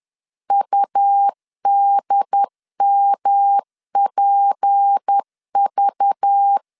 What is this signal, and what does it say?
Morse code, heard as a keyed tone, about 10 wpm, 800 Hz. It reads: UDMPV
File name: CW / DET / 1 9 1500C S A 10wpm 4511